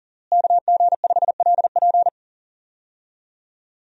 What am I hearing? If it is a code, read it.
Morse code, heard as a keyed tone, about 37 wpm, 700 Hz. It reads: KG5LP
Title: USA Call Signs - 5 Characters - Repeated 3x - 40wpm